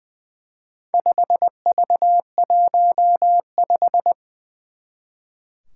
Morse code, heard as a keyed tone, about 20 wpm, 700 Hz. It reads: 5V15
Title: Four Challenging Letter-Number Combinations 20wpm